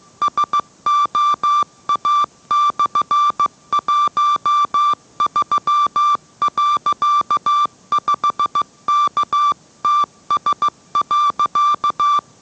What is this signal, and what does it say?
Morse code, heard as a keyed tone, about 15 wpm, 1170 Hz. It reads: SOA/13.5KTS.